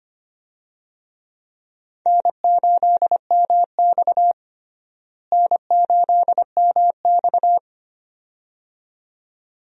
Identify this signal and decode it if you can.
Morse code, heard as a keyed tone, about 25 wpm, 700 Hz. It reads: N8MX N8MX
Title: USA Call Signs - 4 Characters - Repeated 2x - 25wpm